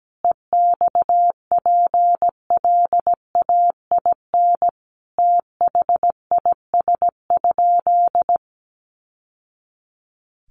Morse code, heard as a keyed tone, about 17 wpm, 700 Hz. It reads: EXPLAIN THIS?
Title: Sentences from Top 1100 Words 17wpm